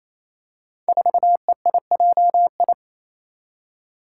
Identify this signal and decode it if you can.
Morse code, heard as a keyed tone, about 29 wpm, 700 Hz. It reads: 4EIJI
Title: Five Challenging Letter-Number Combinations 28wpm